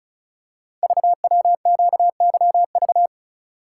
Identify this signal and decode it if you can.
Morse code, heard as a keyed tone, about 36 wpm, 700 Hz. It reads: VWQYV
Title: Code Groups - 5 Letters - 35wpm